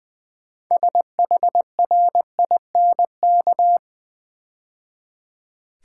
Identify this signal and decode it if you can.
Morse code, heard as a keyed tone, about 20 wpm, 700 Hz. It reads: SHRINK
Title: 6 Letter Words - First 3 Letters Given - 20wpm